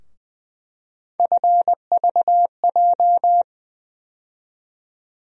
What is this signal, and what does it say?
Morse code, heard as a keyed tone, about 20 wpm, 700 Hz. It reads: FVJ
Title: USA Call Sign Postfixes - 3 Letters 20wpm